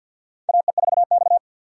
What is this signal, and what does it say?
Morse code, heard as a keyed tone, about 53 wpm, 700 Hz. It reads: AE4X
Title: USA Call Signs - 4 Characters - 3x Character Spacing - 50wpm